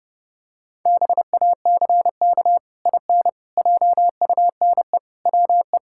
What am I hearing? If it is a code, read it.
Morse code, heard as a keyed tone, about 30 wpm, 700 Hz. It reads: BACK IN JUNE WE
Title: Pangram Challenge v2 - 30wpm